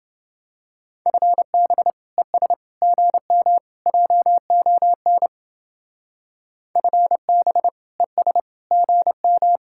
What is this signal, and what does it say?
Morse code, heard as a keyed tone, about 29 wpm, 700 Hz. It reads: FB ES GM JON FB ES GM